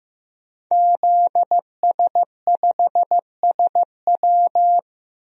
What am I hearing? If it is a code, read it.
Morse code, heard as a keyed tone, about 15 wpm, 700 Hz. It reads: ZS5SW
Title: DXCC-100 Call Signs - 5 Characters - Repeated 2x - 15wpm